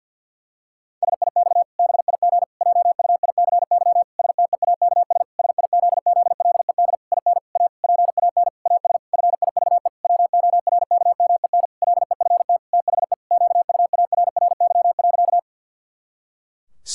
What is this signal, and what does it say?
Morse code, heard as a keyed tone, about 51 wpm, 700 Hz. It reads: SIX BIG JUICY STEAKS SIZZLED IN A PAN AS FIVE WORKMEN LEFT THE QUARRY.
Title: Pangram Challenge v2 - 50wpm